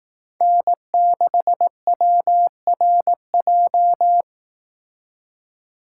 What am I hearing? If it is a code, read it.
Morse code, heard as a keyed tone, about 18 wpm, 700 Hz. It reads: N6WRJ
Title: Speed Racing - USA 1x3 Call Signs 15wpm